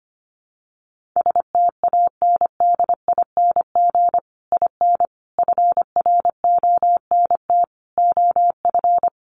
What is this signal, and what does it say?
Morse code, heard as a keyed tone, about 25 wpm, 700 Hz. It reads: STANDING IN FRONT OF
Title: Sets of 5 Words Encore2 25wpm